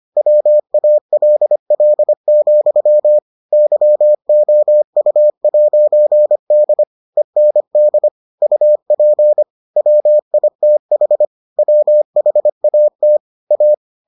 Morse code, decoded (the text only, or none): WALL, YOU'D END UP WITH WHAT A